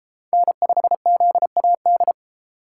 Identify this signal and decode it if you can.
Morse code, heard as a keyed tone, about 32 wpm, 700 Hz. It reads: N5ZAD